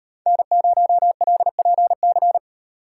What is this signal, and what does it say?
Morse code, heard as a keyed tone, about 38 wpm, 700 Hz. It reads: N0LPC